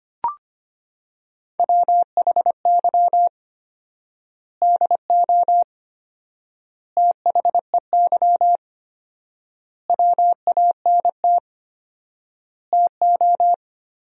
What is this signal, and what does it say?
Morse code, heard as a keyed tone, about 25 wpm, 700 Hz. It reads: WHY DO THEY WANT TO
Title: 4x Word-Spacing - Sentences from Top 400 Words - 25wpm